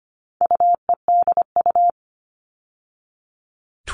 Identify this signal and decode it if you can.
Morse code, heard as a keyed tone, about 25 wpm, 700 Hz. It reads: UEDU
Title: All Cut Numbers - 4-Digit 25wpm